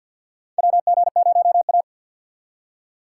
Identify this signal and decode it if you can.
Morse code, heard as a keyed tone, about 53 wpm, 700 Hz. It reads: WG0A
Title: USA Call Signs - 2x Character Spacing - 50wpm